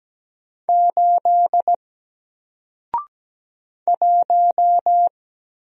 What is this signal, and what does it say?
Morse code, heard as a keyed tone, about 17 wpm, 700 Hz. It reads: 8 1
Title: Single Letter-Number Alt 17wpm